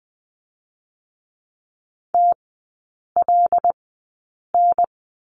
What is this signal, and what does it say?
Morse code, heard as a keyed tone, about 20 wpm, spaced as for 8 wpm, 700 Hz. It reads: TLN
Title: 074 Lesson 13e - Letter L - Three Characters 20wpm